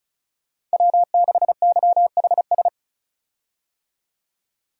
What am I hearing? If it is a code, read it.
Morse code, heard as a keyed tone, about 34 wpm, 700 Hz. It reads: W6YHS
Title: USA Call Signs - 5 Characters - 35wpm